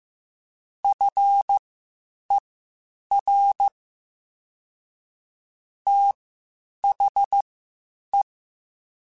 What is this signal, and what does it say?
Morse code, heard as a keyed tone, about 15 wpm, spaced as for 8 wpm, 800 Hz. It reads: FER THE